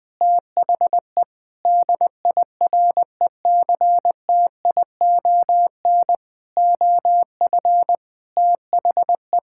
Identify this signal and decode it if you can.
Morse code, heard as a keyed tone, about 20 wpm, 700 Hz. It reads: THE DIRECTION OF THE
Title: Sentences with 5-Word Phrases - 20wpm